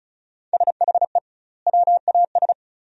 Morse code, heard as a keyed tone, about 36 wpm, 700 Hz. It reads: SHE WAS
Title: 2x Word-Spacing - Sentences from Top 400 Words - 35wpm